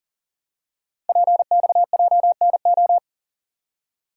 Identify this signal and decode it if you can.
Morse code, heard as a keyed tone, about 38 wpm, 700 Hz. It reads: PXJNO
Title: Code Groups - 5 Letters - 40wpm